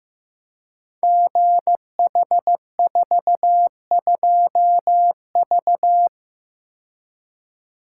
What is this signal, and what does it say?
Morse code, heard as a keyed tone, about 16 wpm, 700 Hz. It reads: GH42V
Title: Code Groups - 5 Letter:Numbers - 15wpm